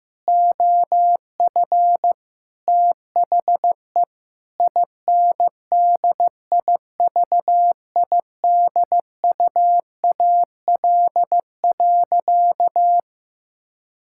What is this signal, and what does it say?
Morse code, heard as a keyed tone, about 15 wpm, 700 Hz. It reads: OF THE INDIVIDUAL.
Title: Sentences from Top 1000 Words - Review Part 6 - 15wpm